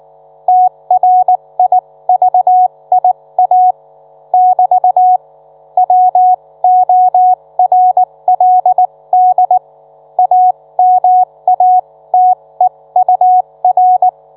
Morse code, heard as a keyed tone, about 19 wpm, spaced as for 16 wpm, 730 Hz. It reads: TRIVIA <BT> WORLD AMATEUR